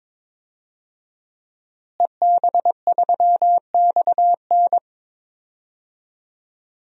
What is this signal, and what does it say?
Morse code, heard as a keyed tone, about 22 wpm, 700 Hz. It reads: EB3XN
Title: DXCC-100 Call Signs - 5 Characters - Repeated 3x - 22wpm